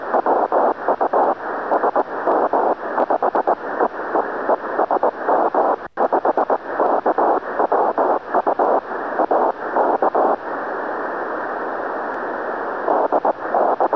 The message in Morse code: none